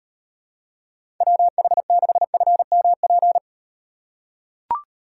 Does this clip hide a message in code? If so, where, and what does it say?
Yes: Morse code from 1.2 to 3.4 s, heard as a keyed tone, about 38 wpm, 700 Hz. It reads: WH6FMP